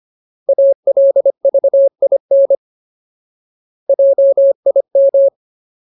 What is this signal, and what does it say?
Morse code, heard as a keyed tone, about 25 wpm, 550 Hz. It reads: ALVIN JIM
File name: Ultimate Word List / 25WPM Ultimate